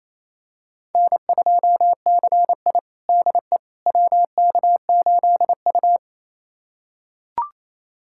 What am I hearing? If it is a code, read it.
Morse code, heard as a keyed tone, about 28 wpm, 700 Hz. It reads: N2CI DE WK8U
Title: USA Callsign DE Callsign - 4 Characters - 28wpm